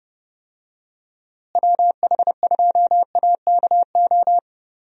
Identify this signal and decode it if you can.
Morse code, heard as a keyed tone, about 30 wpm, 700 Hz. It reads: WH2AKO
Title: USA Call Signs - 6 Characters - Repeated 2x - 30wpm